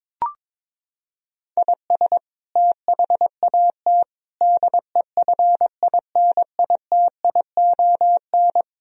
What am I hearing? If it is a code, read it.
Morse code, heard as a keyed tone, about 22 wpm, 700 Hz. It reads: IS THAT DEFINITION
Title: Sentences from Top 1000 Words - Review Part 4 - 22wpm